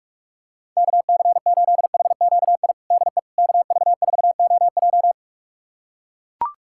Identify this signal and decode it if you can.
Morse code, heard as a keyed tone, about 42 wpm, 700 Hz. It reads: KC8HQI DE KV4OJ
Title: USA Callsign DE Callsign 45wpm